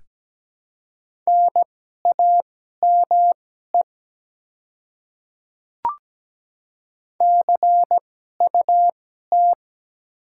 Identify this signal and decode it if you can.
Morse code, heard as a keyed tone, about 17 wpm, spaced as for 12 wpm, 700 Hz. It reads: NAME CUT
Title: 2x Character Spacing - Top 300 Words - 17wpm